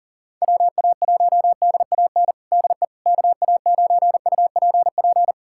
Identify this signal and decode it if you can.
Morse code, heard as a keyed tone, about 42 wpm, 700 Hz. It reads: WA1DAN DE KA9UPP
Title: USA Callsign DE Callsign 40wpm